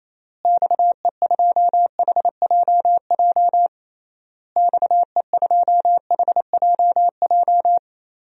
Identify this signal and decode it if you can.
Morse code, heard as a keyed tone, about 28 wpm, 700 Hz. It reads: XE2HJJ XE2HJJ
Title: DXCC-100 Call Signs - 6 Characters - Repeated 3x - 28wpm